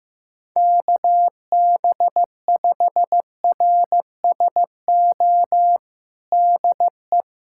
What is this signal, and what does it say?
Morse code, heard as a keyed tone, about 15 wpm, 700 Hz. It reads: KB5RSO DE
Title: USA Callsign DE Callsign - 6 Characters - 15wpm